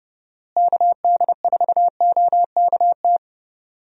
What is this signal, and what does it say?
Morse code, heard as a keyed tone, about 30 wpm, 700 Hz. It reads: KD4OKT
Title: USA Call Signs - 6 Characters - Repeated 3x - 30wpm